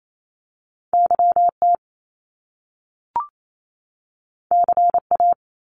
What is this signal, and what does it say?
Morse code, heard as a keyed tone, about 28 wpm, 700 Hz. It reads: Y T C A
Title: Top Two Letter Combinations in Words 28wpm